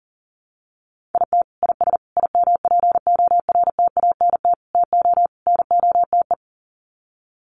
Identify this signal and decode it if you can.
Morse code, heard as a keyed tone, about 40 wpm, 700 Hz. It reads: IT IS IMPORTANT TO NOTE
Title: Sets of 5 Words 40wpm